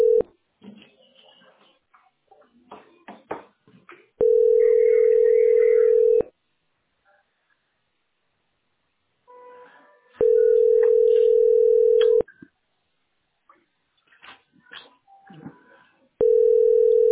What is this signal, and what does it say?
Ringback tone (North America), 440 + 480 Hz.